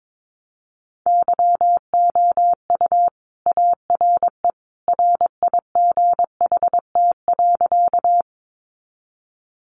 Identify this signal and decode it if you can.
Morse code, heard as a keyed tone, about 22 wpm, 700 Hz. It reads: YOU ARE RIGHT.
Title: Sentences from Top 200 Words 22wpm